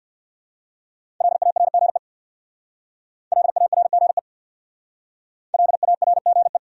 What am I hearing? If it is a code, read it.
Morse code, heard as a keyed tone, about 57 wpm, 700 Hz. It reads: LARGE LARGE LARGE
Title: Speed-Racing Top 300 Words 50wpm